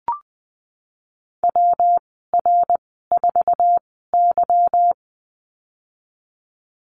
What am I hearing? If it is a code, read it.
Morse code, heard as a keyed tone, about 20 wpm, spaced as for 15 wpm, 700 Hz. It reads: WR4Y
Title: 186 Lesson 30g - Letter b - Call Signs 20wpm